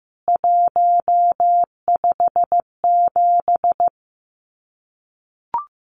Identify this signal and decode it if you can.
Morse code, heard as a keyed tone, about 15 wpm, 700 Hz. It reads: 157
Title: CWT Exchange 15wpm